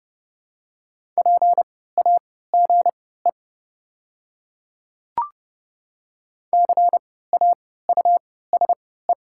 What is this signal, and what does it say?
Morse code, heard as a keyed tone, about 30 wpm, spaced as for 17 wpm, 700 Hz. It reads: PAGE CAUSE
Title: 3x Character Spacing - Top 500 Words - 30wpm